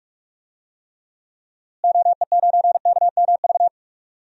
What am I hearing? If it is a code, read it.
Morse code, heard as a keyed tone, about 46 wpm, 700 Hz. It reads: OE9KMV
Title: DXCC-100 Call Signs - 6 Characters - Repeated 3x - 45wpm